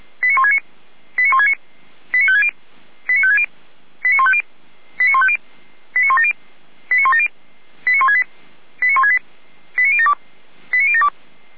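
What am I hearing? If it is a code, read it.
Five-tone selective call (ZVEI), heard as quick runs of five tones, tones of about 70 ms. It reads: 89169 89169 89560 89560 89150 89150 89170 89170 89168 89168 89072 89072
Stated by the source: ZVEI selective call 89223 received on an SDR by the Orlamünde fire brigade.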